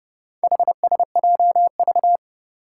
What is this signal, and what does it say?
Morse code, heard as a keyed tone, about 30 wpm, 700 Hz. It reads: HSJV
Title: Four Challenging Letter-Number Combinations 30wpm